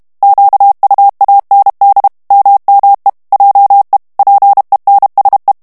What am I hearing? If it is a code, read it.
Morse code, heard as a keyed tone, about 32 wpm, 800 Hz. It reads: QUAND MME JE PENSE
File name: morse mp3 / FRANCAIS TEXTE CLAIR 1 VITESSE 30 MOTS MINUTES